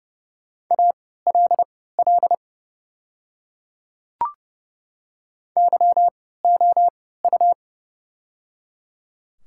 Morse code, heard as a keyed tone, about 30 wpm, spaced as for 17 wpm, 700 Hz. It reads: ALL YOU
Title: 3x Character Spacing - Top 300 Words - 30wpm